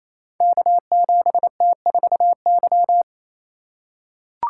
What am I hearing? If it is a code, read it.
Morse code, heard as a keyed tone, about 27 wpm, 700 Hz. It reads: K7T4Y
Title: Code Groups - 5 Letter:Numbers - 28wpm